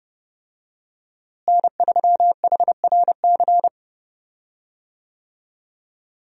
Morse code, heard as a keyed tone, about 30 wpm, 700 Hz. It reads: N3HRC
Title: USA Call Signs - 5 Characters - Repeated 3x - 30wpm